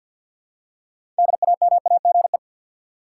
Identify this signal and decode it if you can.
Morse code, heard as a keyed tone, about 50 wpm, 700 Hz. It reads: DAMAGE
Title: Six Letter Words - 50wpm